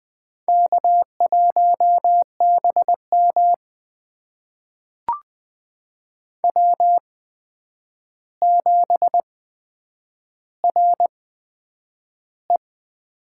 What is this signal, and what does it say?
Morse code, heard as a keyed tone, about 20 wpm, spaced as for 9 wpm, 700 Hz. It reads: K1BM W7RE
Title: USA Call Signs - 4 Characters - 8x Character Spacing - 20wpm